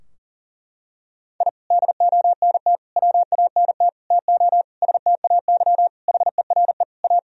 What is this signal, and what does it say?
Morse code, heard as a keyed tone, about 40 wpm, 700 Hz. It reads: I DONT WANT TO STAY HERE A
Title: Sentences from Top 1000 Words - Review Part 4 - 40wpm